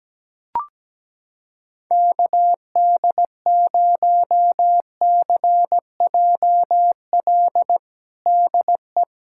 Morse code, heard as a keyed tone, about 17 wpm, 700 Hz. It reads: KD0CJL DE
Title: USA Callsign DE Callsign 17wpm